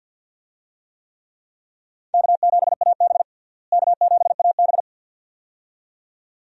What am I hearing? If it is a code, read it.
Morse code, heard as a keyed tone, about 52 wpm, 700 Hz. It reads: K7AB K7AB
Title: USA Call Signs - 4 Characters - Repeated 2x - 50wpm